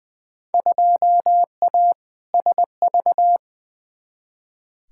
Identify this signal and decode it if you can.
Morse code, heard as a keyed tone, about 20 wpm, 700 Hz. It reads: 2A SV
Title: Field Day Exchanges - 20wpm